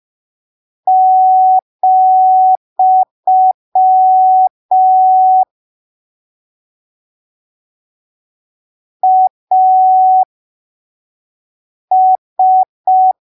Morse code, heard as a keyed tone, about 5 wpm, spaced as for 3 wpm, 750 Hz. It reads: , AS
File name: arquivos / texto / texto 25 05ppm